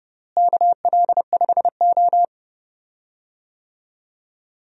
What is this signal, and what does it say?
Morse code, heard as a keyed tone, about 30 wpm, 700 Hz. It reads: KL5O
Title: USA Call Signs - 4 Characters - Repeated 3x - 30wpm